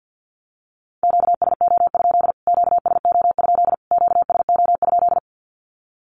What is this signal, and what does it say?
Morse code, heard as a keyed tone, about 50 wpm, 700 Hz. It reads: QSO? QSO? QSO?
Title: Q-Codes Repeated 3x 50wpm